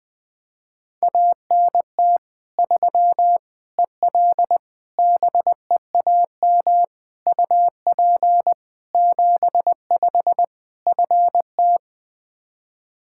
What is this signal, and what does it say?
Morse code, heard as a keyed tone, about 21 wpm, 700 Hz. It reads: ANT 3 EL BEAM UP 75 FT
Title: Practice QSO - Exchange 2 - Info - 20wpm